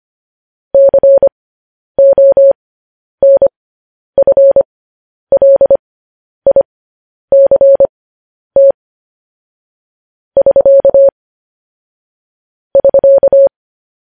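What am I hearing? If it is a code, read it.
Morse code, heard as a keyed tone, about 25 wpm, spaced as for 10 wpm, 550 Hz. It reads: CONFLICT <SK> <SK>
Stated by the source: Morse code transmission of news headlines.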